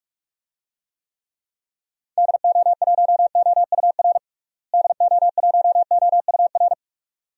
Morse code, heard as a keyed tone, about 44 wpm, 700 Hz. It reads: DO1OUR DO1OUR
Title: DXCC-100 Call Signs - 6 Characters - Repeated 2x - 45wpm